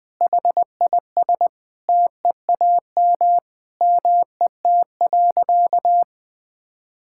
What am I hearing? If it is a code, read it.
Morse code, heard as a keyed tone, about 21 wpm, 700 Hz. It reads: HIS TEAM MET.
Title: Sentences with 5-Word Phrases - 20wpm